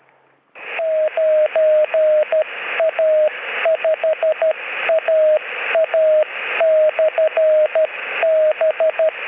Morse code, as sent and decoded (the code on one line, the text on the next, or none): ----. .- ..... .- .- -..-. -...
9A5AA/B